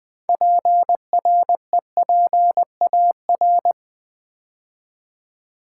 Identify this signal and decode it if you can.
Morse code, heard as a keyed tone, about 20 wpm, 700 Hz. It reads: PREPAR
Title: Build A Word - 2 Letters At A Time - 8-Letter Words - 20wpm